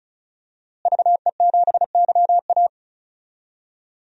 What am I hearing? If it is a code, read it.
Morse code, heard as a keyed tone, about 34 wpm, 700 Hz. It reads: VE7YA
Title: DXCC-100 Call Signs - 5 Characters - Repeated 2x - 35wpm